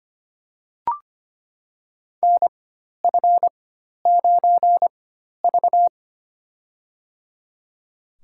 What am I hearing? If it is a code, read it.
Morse code, heard as a keyed tone, about 25 wpm, spaced as for 12 wpm, 700 Hz. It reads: NF9V
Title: USA Call Signs - 4 Characters - 4x Character Spacing - 25wpm